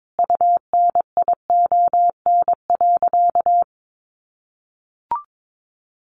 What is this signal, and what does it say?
Morse code, heard as a keyed tone, about 22 wpm, 700 Hz. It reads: UNION.
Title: Sentences from Top 1400 Words 22wpm